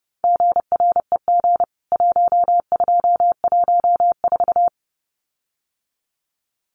Morse code, heard as a keyed tone, about 30 wpm, 700 Hz. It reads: GREG 1214
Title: CWT Exchange 30wpm